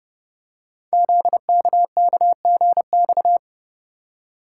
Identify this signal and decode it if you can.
Morse code, heard as a keyed tone, about 31 wpm, 700 Hz. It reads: ZKKGX
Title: Code Groups - 5 Letter:Numbers - 30wpm